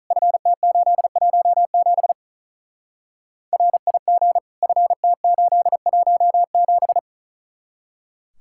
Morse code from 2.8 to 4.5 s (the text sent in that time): RIG